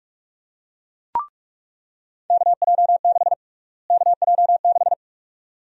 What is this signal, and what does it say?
Morse code, heard as a keyed tone, about 46 wpm, 700 Hz. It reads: KJ6 KJ6
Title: USA Call Sign Prefixes - Repeated 2x - 45wpm